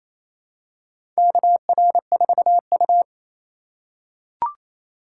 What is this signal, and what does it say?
Morse code, heard as a keyed tone, about 28 wpm, 700 Hz. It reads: KR4U
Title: USA Call Signs - 4 Characters - 4x Character Spacing - 28wpm